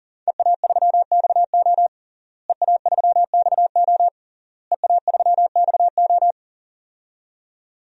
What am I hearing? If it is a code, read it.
Morse code, heard as a keyed tone, about 40 wpm, 700 Hz. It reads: EA3XO EA3XO EA3XO
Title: DXCC-100 Call Signs - 5 Characters - Repeated 3x - 40wpm